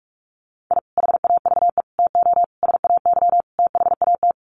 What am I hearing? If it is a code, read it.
Morse code, heard as a keyed tone, about 44 wpm, 700 Hz. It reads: I HAVE TO SAY THAT
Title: Sets of 5 Words Encore 45wpm